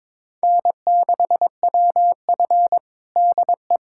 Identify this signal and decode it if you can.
Morse code, heard as a keyed tone, about 22 wpm, 700 Hz. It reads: N6WF DE
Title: USA Callsign DE Callsign - 4 Characters - 22wpm